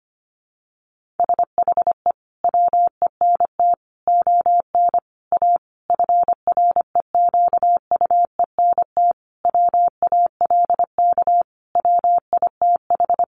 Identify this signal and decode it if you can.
Morse code, heard as a keyed tone, about 25 wpm, 700 Hz. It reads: SHE WENT ON A FREQUENT WALK WITH
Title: Sentences from Top 1400 Words 25wpm